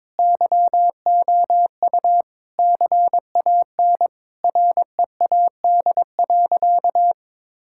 Morse code, heard as a keyed tone, about 22 wpm, 700 Hz. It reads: YOU CAN READ.
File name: practice / 22wpm / Sentences with 8-Word Phrases - 22wpm